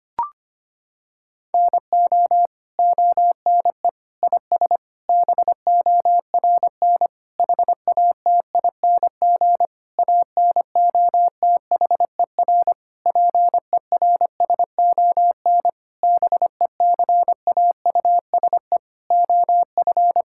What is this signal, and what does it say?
Morse code, heard as a keyed tone, about 25 wpm, 700 Hz. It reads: NO ONE IS BORN HATING ANOTHER PERSON BECAUSE OF
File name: practice / 25wpm / Sentences with 9-Word Phrases - 25wpm